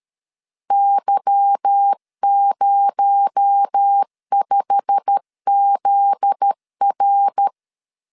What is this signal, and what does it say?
Morse code, heard as a keyed tone, about 13 wpm, 800 Hz. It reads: Y05ZR
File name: LEZIONI / 10 SIGNS / 10A SLASH 12wpm 2314